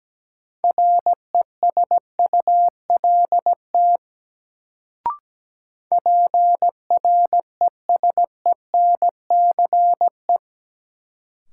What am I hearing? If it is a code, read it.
Morse code, heard as a keyed tone, about 17 wpm, 700 Hz. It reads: RESULT PRESENCE
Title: Sets of 3 Words - New Word Drill - 17wpm